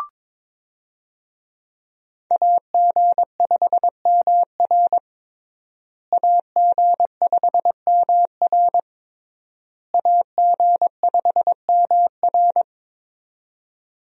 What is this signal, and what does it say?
Morse code, heard as a keyed tone, about 21 wpm, 700 Hz. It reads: AG5MR AG5MR AG5MR
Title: USA Call Signs - 5 Characters - Repeated 3x - 22wpm